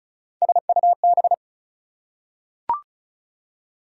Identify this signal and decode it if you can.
Morse code, heard as a keyed tone, about 36 wpm, 700 Hz. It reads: SUB
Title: Prefixes and Suffixes 35wpm